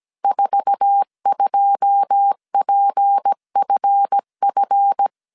Morse code, heard as a keyed tone, about 17 wpm, 800 Hz. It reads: 42PFF
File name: CW / DET / 1 9 1500C S A 16wpm 2828